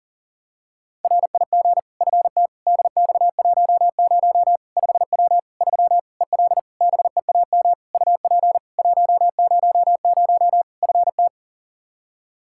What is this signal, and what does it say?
Morse code, heard as a keyed tone, about 40 wpm, 700 Hz. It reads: RIG FT DX10 5W 3 EL BEAM UP 100 FT
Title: Practice QSO - Exchange 2 - Info - 40wpm